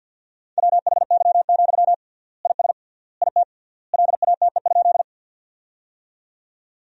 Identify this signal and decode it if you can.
Morse code, heard as a keyed tone, about 50 wpm, 700 Hz. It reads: WHY, IS IT LATE?
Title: 3x Word-Spacing - Sentences from Top 300 Words - 50wpm